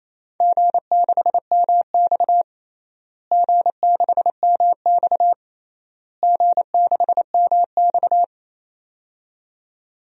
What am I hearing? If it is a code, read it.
Morse code, heard as a keyed tone, about 28 wpm, 700 Hz. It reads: G6MX G6MX G6MX
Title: DXCC-100 Call Signs - 4 Characters - Repeated 3x - 28wpm